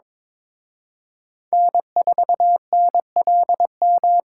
Morse code, heard as a keyed tone, about 26 wpm, 700 Hz. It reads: N4NLM
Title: Speed Racing - USA 1x3 Call Signs 22wpm